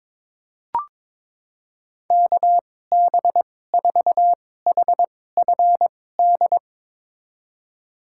Morse code, heard as a keyed tone, about 22 wpm, spaced as for 16 wpm, 700 Hz. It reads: KB4HFD
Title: USA Call Signs - 2x Character Spacing - 22wpm